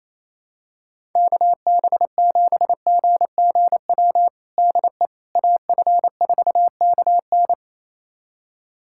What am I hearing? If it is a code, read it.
Morse code, heard as a keyed tone, about 28 wpm, 700 Hz. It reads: KB7GGW DE AF4KN
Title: USA Callsign DE Callsign 28wpm